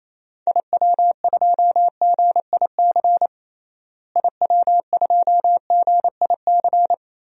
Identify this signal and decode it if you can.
Morse code, heard as a keyed tone, about 28 wpm, 700 Hz. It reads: IW2GIC IW2GIC
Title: DXCC-100 Call Signs - 6 Characters - Repeated 2x - 28wpm